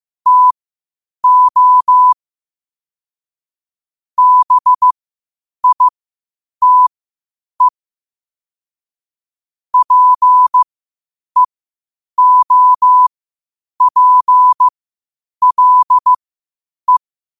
Morse code, heard as a keyed tone, about 15 wpm, spaced as for 8 wpm, 1000 Hz. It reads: TO BITE PEOPLE
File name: qotdfiles / 8wpm